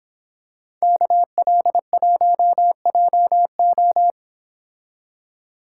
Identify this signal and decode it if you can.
Morse code, heard as a keyed tone, about 25 wpm, 700 Hz. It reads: KL1JO